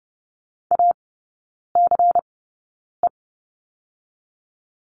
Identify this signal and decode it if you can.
Morse code, heard as a keyed tone, about 30 wpm, spaced as for 9 wpm, 700 Hz. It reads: ACE
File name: Lesson 16e - Letter c - Three Characters 30wpm